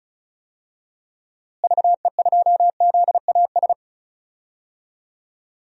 Morse code, heard as a keyed tone, about 36 wpm, 700 Hz. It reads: VE2ZAS
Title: DXCC-100 Call Signs - 6 Characters - Repeated 2x - 35wpm